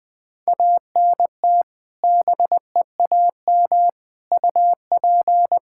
Morse code, heard as a keyed tone, about 20 wpm, 700 Hz. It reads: ANT BEAM UP
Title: K1USN SST QSO Practice 20wpm